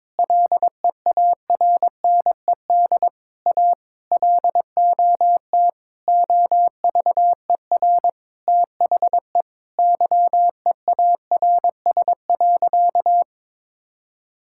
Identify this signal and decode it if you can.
Morse code, heard as a keyed tone, about 22 wpm, 700 Hz. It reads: LEARNED A LOT OVER THE YEARS.
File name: practice / 22wpm / Sentences with 3-Word Phrases - 22wpm